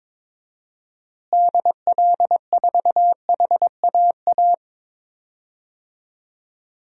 Morse code, heard as a keyed tone, about 21 wpm, 700 Hz. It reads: DL4HAA
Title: DXCC-100 Call Signs - 6 Characters - Repeated 3x - 22wpm